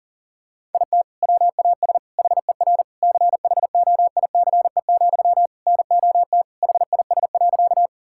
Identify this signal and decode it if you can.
Morse code, heard as a keyed tone, about 37 wpm, 700 Hz. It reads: IT WAS HER CHOICE, NOT HIS.
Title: Sentences from Top 1000 Words - Review Part 6 - 40wpm